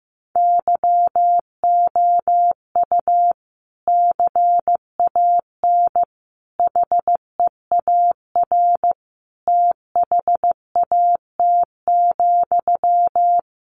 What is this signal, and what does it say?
Morse code, heard as a keyed tone, about 15 wpm, 700 Hz. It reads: YOU CAN HEAR THAT,
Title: Sentences from Top 500 Words 15wpm